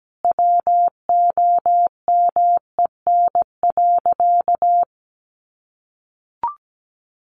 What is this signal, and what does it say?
Morse code, heard as a keyed tone, about 17 wpm, 700 Hz. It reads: WOMEN.
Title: Sentences from Top 1400 Words 17wpm